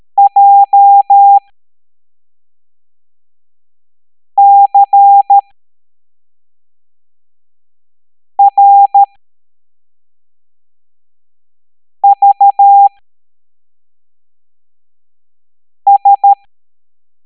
Morse code, heard as a keyed tone, about 13 wpm, spaced as for 3 wpm, 800 Hz. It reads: JCRVS
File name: hamdata / L09-EX04